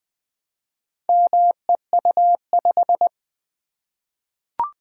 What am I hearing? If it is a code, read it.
Morse code, heard as a keyed tone, about 20 wpm, 700 Hz. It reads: MEU5
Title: Four Letter-Number Combinations - 20wpm